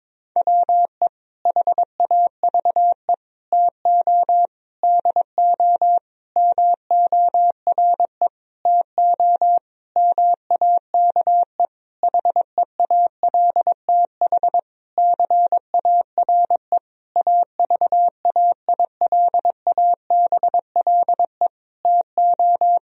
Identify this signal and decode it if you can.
Morse code, heard as a keyed tone, about 22 wpm, 700 Hz. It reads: WE HAVE TO DO MORE TO MAKE HEALTH CARE AVAILABLE TO
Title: Sentences from Top 1000 Words - Review Part 3 - 22wpm